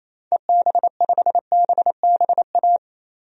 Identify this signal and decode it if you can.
Morse code, heard as a keyed tone, about 27 wpm, 700 Hz. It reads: EB5BBA
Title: DXCC-100 Call Signs - 6 Characters - Repeated 3x - 28wpm